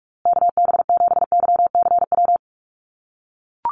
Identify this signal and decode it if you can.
Morse code, heard as a keyed tone, about 44 wpm, 700 Hz. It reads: KB7YCW
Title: USA Call Signs Repeated 2x 45wpm